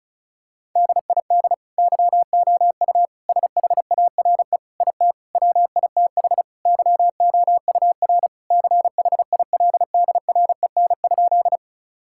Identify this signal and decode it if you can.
Morse code, heard as a keyed tone, about 36 wpm, 700 Hz. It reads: DID YOU SHARE IT WITH YOUR CHILDREN?
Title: Sentences from Top 1000 Words - Review Part 6 - 35wpm